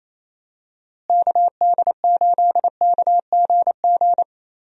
Morse code, heard as a keyed tone, about 28 wpm, 700 Hz. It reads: KD8KGG
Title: USA Call Signs - 6 Characters - Repeated 3x - 28wpm